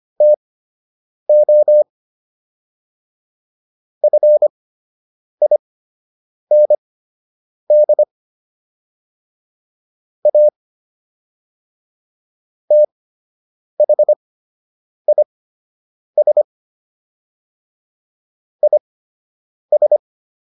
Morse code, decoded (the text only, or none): TO FIND A THIS IS